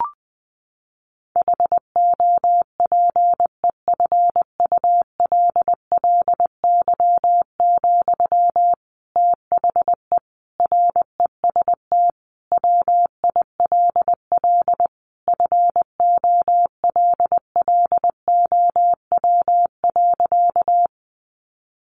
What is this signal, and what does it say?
Morse code, heard as a keyed tone, about 20 wpm, 700 Hz. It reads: HOPEFULLY, THE REST WILL FOLLOW.
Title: Sentences from Top 1500 Words 20wpm